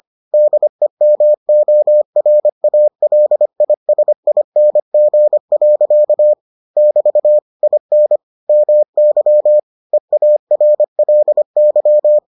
Morse code, heard as a keyed tone, about 25 wpm, 600 Hz. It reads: DEMORALISING. <BT> IN MY EARLY